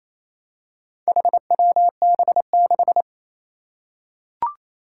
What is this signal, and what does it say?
Morse code, heard as a keyed tone, about 28 wpm, 700 Hz. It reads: HWB6
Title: Four Challenging Letter-Number Combinations 28wpm